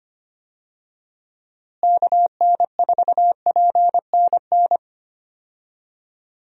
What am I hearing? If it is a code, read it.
Morse code, heard as a keyed tone, about 25 wpm, 700 Hz. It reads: KN4PNN
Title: USA Call Signs - 6 Characters - Repeated 3x - 25wpm